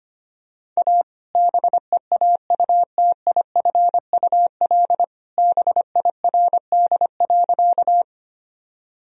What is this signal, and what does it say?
Morse code, heard as a keyed tone, about 25 wpm, 700 Hz. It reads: A BEAUTIFUL BIRD.
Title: Sentences from Top 1000 Words - Review Part 2 - 25wpm